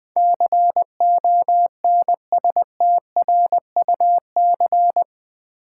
Morse code, heard as a keyed tone, about 19 wpm, 700 Hz. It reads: CONSTRUC
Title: Build A Word - 4 Letters At A Time - 12-Letter Words - 20wpm